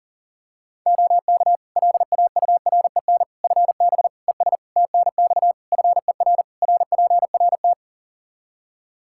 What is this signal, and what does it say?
Morse code, heard as a keyed tone, about 40 wpm, 700 Hz. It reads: OK LAUREN FB ES TNX FER RPRT
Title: Practice QSO - Exchange 2 - Info - 40wpm